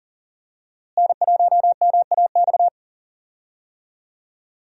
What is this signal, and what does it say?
Morse code, heard as a keyed tone, about 38 wpm, 700 Hz. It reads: N1MAX
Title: USA Call Signs - 5 Characters - Repeated 3x - 40wpm